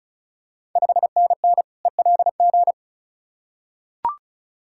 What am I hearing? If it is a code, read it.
Morse code, heard as a keyed tone, about 36 wpm, 690 Hz. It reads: 5NN ELG
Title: Ontario QSO Party Exchange 35wpm